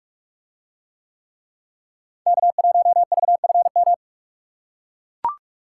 Morse code, heard as a keyed tone, about 44 wpm, 700 Hz. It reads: K1VFM